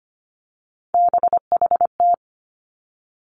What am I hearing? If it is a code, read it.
Morse code, heard as a keyed tone, about 25 wpm, 700 Hz. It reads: BHT
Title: Three Challenging Letter-Number Combinations 25wpm